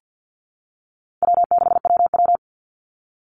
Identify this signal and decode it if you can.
Morse code, heard as a keyed tone, about 49 wpm, 700 Hz. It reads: W6WW
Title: NA SOTA Chaser Callsigns 50wpm